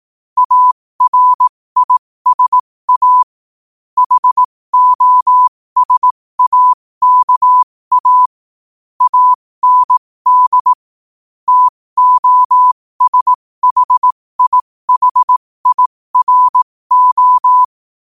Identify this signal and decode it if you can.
Morse code, heard as a keyed tone, about 18 wpm, spaced as for 15 wpm, 1000 Hz. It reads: ARISA HOSAKA AND TOSHIHIRO
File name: qotdfiles / 15wpm